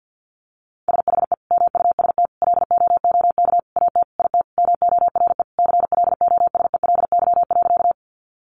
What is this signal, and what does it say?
Morse code, heard as a keyed tone, about 50 wpm, 700 Hz. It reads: SHE MUST LOOK AT IT MORE CLOSELY.
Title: Sentences from Top 1400 Words 50wpm